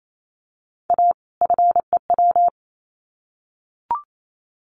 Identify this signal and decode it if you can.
Morse code, heard as a keyed tone, about 28 wpm, 700 Hz. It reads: A FEW
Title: Sets of 2 Words 28wpm